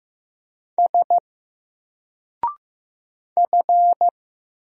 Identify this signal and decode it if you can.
Morse code, heard as a keyed tone, about 15 wpm, 700 Hz. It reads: S F
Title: Single Character (Letter:Number:Punctuation:Prosign) 15wpm